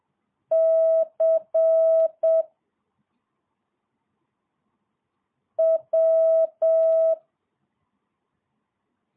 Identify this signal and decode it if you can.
Morse code, heard as a keyed tone, about 7 wpm, 660 Hz. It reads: C W
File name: jh8cbh / 2022cw